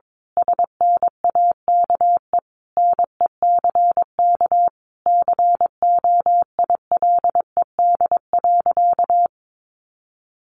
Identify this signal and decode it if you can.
Morse code, heard as a keyed tone, about 22 wpm, 700 Hz. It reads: SNAKE NECK COILED.
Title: A Sound Of Thunder 22wpm